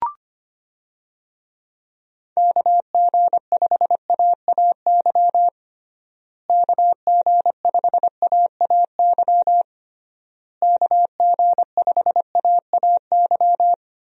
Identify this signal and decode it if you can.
Morse code, heard as a keyed tone, about 25 wpm, 700 Hz. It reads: KG5AAY KG5AAY KG5AAY
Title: USA Call Signs - 6 Characters - Repeated 3x - 25wpm